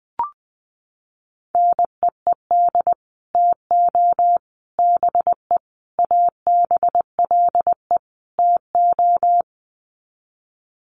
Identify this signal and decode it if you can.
Morse code, heard as a keyed tone, about 20 wpm, 700 Hz. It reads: NEED TO BE ABLE TO
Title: Sets of 5 Words Encore 20wpm